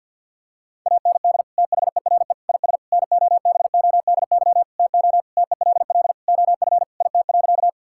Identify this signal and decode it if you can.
Morse code, heard as a keyed tone, about 50 wpm, 700 Hz. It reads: AND THERE IS NOBODY TO TELL OF IT.
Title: Sentences from Top 1600 Words 50wpm